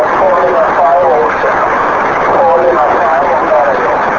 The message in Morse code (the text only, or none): none